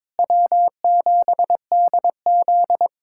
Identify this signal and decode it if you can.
Morse code, heard as a keyed tone, about 22 wpm, 700 Hz. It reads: W7DZ
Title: USA Callsign DE Callsign - 4 Characters - 22wpm